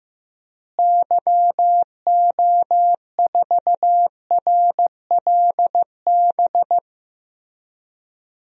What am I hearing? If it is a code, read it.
Morse code, heard as a keyed tone, about 15 wpm, 700 Hz. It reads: YO4RLB
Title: DXCC-100 Call Signs - 6 Characters - Repeated 2x - 15wpm